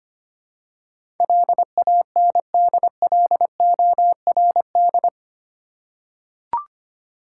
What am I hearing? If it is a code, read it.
Morse code, heard as a keyed tone, about 25 wpm, 700 Hz. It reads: LANDLORD
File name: 8 Letter Words - First 3 Letters Given - 25wpm